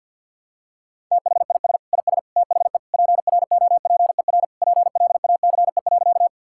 Morse code, heard as a keyed tone, about 52 wpm, 700 Hz. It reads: THIS IS THE PROPER PLACE.
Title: Sentences from Top 1000 Words - Review Part 5 - 50wpm